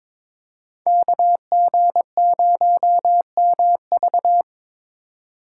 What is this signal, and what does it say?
Morse code, heard as a keyed tone, about 26 wpm, 700 Hz. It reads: KG0MV
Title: Speed Racing - USA 2x2 Call Signs 22wpm